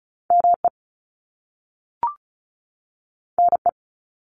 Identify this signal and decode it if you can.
Morse code, heard as a keyed tone, about 36 wpm, 700 Hz. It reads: ME NE
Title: US States 35wpm